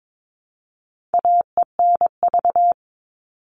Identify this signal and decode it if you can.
Morse code, heard as a keyed tone, about 22 wpm, 700 Hz. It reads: AENV
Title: All Cut Numbers - 4-Digit 22wpm